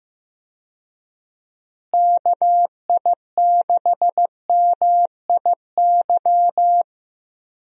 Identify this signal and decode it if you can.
Morse code, heard as a keyed tone, about 15 wpm, 700 Hz. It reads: KI6MIY
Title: USA Call Signs - 6 Characters - Repeated 2x - 15wpm